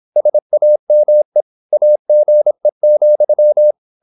Morse code, heard as a keyed tone, about 26 wpm, 600 Hz. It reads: SAME AGE,
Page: Never 26wpm | CW med Gnister